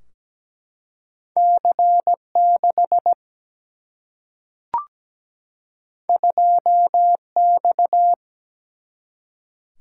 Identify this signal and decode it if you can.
Morse code, heard as a keyed tone, about 17 wpm, 700 Hz. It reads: C 6 2 X
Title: Single-Letter Single-Number 17wpm